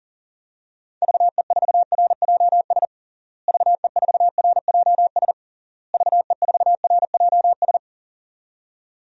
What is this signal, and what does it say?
Morse code, heard as a keyed tone, about 40 wpm, 700 Hz. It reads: VE4RJS VE4RJS VE4RJS
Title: DXCC-100 Call Signs - 6 Characters - Repeated 3x - 40wpm